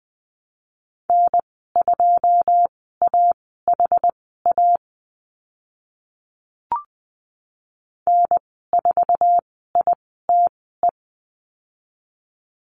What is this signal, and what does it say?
Morse code, heard as a keyed tone, about 20 wpm, spaced as for 15 wpm, 700 Hz. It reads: N2AHA N4ITE
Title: 088 Lesson 15g - Number 5 - Call Signs 20wpm